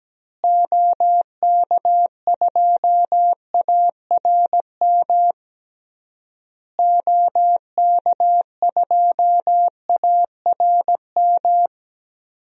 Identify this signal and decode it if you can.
Morse code, heard as a keyed tone, about 17 wpm, 700 Hz. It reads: OK2ARM OK2ARM
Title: DXCC-100 Call Signs - 6 Characters - Repeated 3x - 17wpm